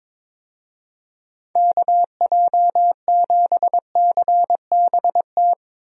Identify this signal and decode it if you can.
Morse code, heard as a keyed tone, about 22 wpm, 700 Hz. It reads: KJ7CBT
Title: USA Call Signs - 6 Characters - 22wpm